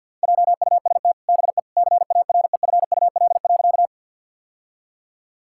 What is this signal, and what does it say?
Morse code, heard as a keyed tone, about 50 wpm, 700 Hz. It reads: JUST BE CAREFUL.
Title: Sentences from Top 1600 Words 50wpm